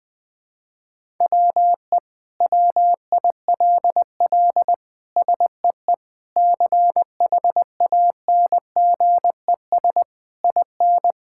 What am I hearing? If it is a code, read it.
Morse code, heard as a keyed tone, about 20 wpm, 700 Hz. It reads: WE WILL SEE CHANGES IN
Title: Sentences with 7-Word Phrases - Spoken Beginning - 20wpm